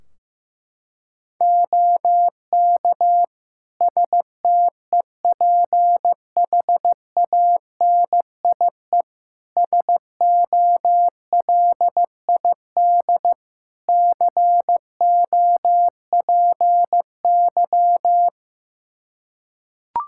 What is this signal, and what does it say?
Morse code, heard as a keyed tone, about 15 wpm, 700 Hz. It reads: OK STEPHANIE SOLID COPY
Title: Practice QSO - Exchange 3 - More Info - 15wpm